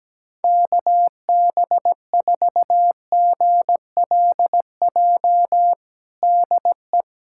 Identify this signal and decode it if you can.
Morse code, heard as a keyed tone, about 17 wpm, 700 Hz. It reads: KB4GLJ DE
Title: USA Callsign DE Callsign - 6 Characters - 17wpm